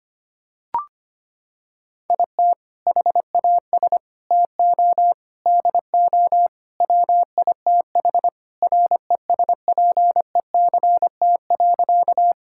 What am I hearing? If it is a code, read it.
Morse code, heard as a keyed tone, about 25 wpm, 700 Hz. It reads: IT HAS TO DO WITH RESPECT.
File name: practice / 25wpm / Sentences with 4-Word Phrases - 25wpm